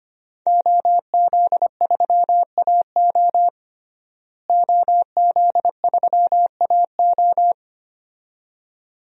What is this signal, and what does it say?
Morse code, heard as a keyed tone, about 25 wpm, 700 Hz. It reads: OZ3AO OZ3AO
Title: DXCC-100 Call Signs - 5 Characters - Repeated 2x - 25wpm